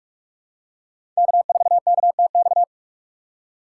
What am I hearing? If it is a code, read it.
Morse code, heard as a keyed tone, about 42 wpm, 690 Hz. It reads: K4KTX
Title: USA Call Signs - 5 Characters - 6x Character Spacing - 45wpm